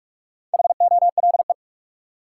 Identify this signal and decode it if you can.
Morse code, heard as a keyed tone, about 46 wpm, 700 Hz. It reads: HOLE
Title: Four Letter Words - 45wpm